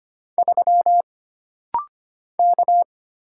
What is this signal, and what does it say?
Morse code, heard as a keyed tone, about 25 wpm, 700 Hz. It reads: 3 K
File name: Single Character (Letter:Number:Punctuation:Prosign) - Rapid-Fire - 25wpm